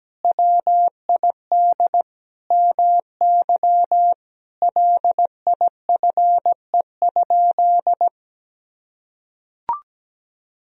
Morse code, heard as a keyed tone, about 17 wpm, 700 Hz. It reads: WID MY LIFE?
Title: Sentences from Top 300 Words and Abbreviations 17wpm